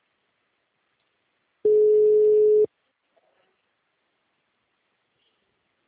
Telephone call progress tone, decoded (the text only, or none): ringback tone